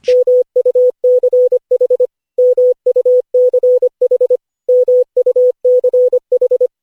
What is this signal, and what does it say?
Morse code, heard as a keyed tone, about 25 wpm, 500 Hz. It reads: MUCH MUCH MUCH
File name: cw / Session5 cw training